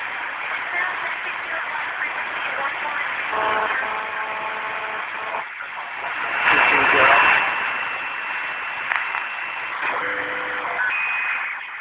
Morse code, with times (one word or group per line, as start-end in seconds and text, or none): none